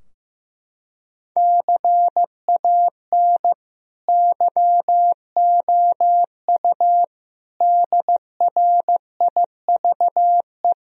Morse code, heard as a keyed tone, about 16 wpm, 700 Hz. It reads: CAN YOU DRIVE
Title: Sentences from Top 1000 Words - Review Part 2 - 15wpm